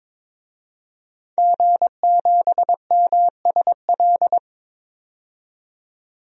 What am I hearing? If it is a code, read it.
Morse code, heard as a keyed tone, about 22 wpm, 700 Hz. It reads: G7MSL
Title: DXCC-100 Call Signs - 5 Characters - 22wpm